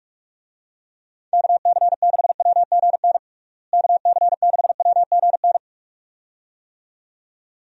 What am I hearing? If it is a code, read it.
Morse code, heard as a keyed tone, about 42 wpm, 700 Hz. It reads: KC6WGN KC6WGN
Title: USA Call Signs - 6 Characters - Repeated 2x - 45wpm